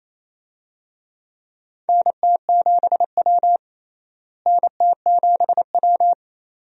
Morse code, heard as a keyed tone, about 28 wpm, 700 Hz. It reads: NT7W NT7W
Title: USA Call Signs - 4 Characters - Repeated 2x - 28wpm